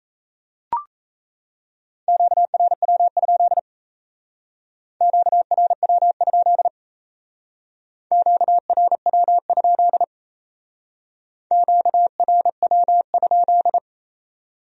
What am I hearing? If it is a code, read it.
Morse code, heard as a keyed tone, about 35 wpm, 700 Hz. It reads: QRW? QRW? QRW? QRW?